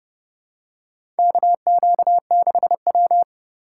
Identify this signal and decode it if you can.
Morse code, heard as a keyed tone, about 31 wpm, 700 Hz. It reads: KQ6W
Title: USA Call Signs - 4 Characters - 2x Character Spacing - 30wpm